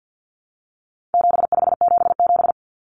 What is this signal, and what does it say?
Morse code, heard as a keyed tone, about 50 wpm, 700 Hz. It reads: 7577
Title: Common Cut Numbers - 4-Digit 50wpm